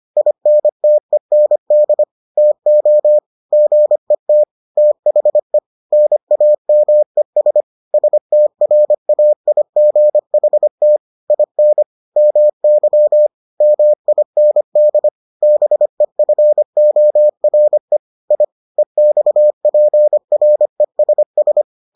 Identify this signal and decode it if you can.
Morse code, heard as a keyed tone, about 25 wpm, 600 Hz. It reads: INTEND TO GET THE NAMES STRAIGHT IN MY MIND BEFORE I EXPRESS